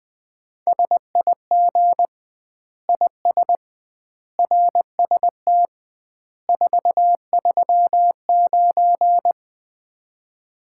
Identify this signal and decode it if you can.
Morse code, heard as a keyed tone, about 21 wpm, 700 Hz. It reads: SIG IS RST 439
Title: Common QSO Phrases - 2x Word-Spacing - 20wpm